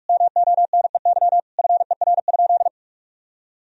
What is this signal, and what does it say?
Morse code, heard as a keyed tone, about 46 wpm, 700 Hz. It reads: MONEY FER?
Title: Sentences from Top 400 Words and Abbreviations 45wpm